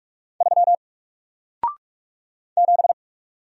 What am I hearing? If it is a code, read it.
Morse code, heard as a keyed tone, about 44 wpm, 700 Hz. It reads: 3 7
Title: Single Letter-Number Alt 45wpm